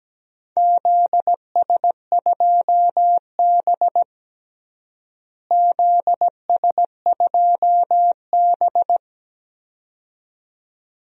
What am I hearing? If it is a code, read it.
Morse code, heard as a keyed tone, about 17 wpm, 700 Hz. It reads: ZS2B ZS2B
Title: DXCC-100 Call Signs - 4 Characters - Repeated 2x - 17wpm